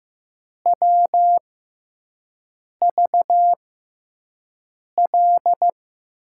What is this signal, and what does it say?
Morse code, heard as a keyed tone, about 15 wpm, spaced as for 5 wpm, 700 Hz. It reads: WVL
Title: USA Call Signs - 6x Character Spacing - 15wpm